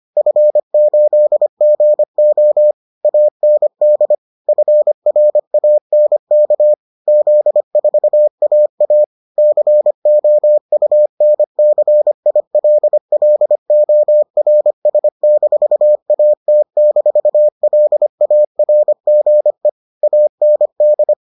Morse code, heard as a keyed tone, about 25 wpm, 600 Hz. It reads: F8GO AND FRANK Z4AA COUNCILLORS-AT-LARGE AND